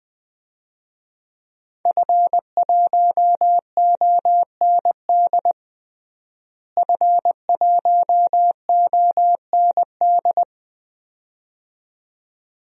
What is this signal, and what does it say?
Morse code, heard as a keyed tone, about 21 wpm, 700 Hz. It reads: F1OND F1OND
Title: DXCC-100 Call Signs - 5 Characters - Repeated 2x - 20wpm